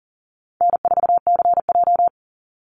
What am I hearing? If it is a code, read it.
Morse code, heard as a keyed tone, about 40 wpm, 700 Hz. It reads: N4CJ
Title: Easy Call Signs - 40wpm